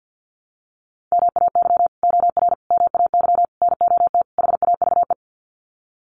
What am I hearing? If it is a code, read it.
Morse code, heard as a keyed tone, about 49 wpm, 700 Hz. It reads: MAY OR MAY NOT HAVE
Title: Sets of 5 Words Encore2 50wpm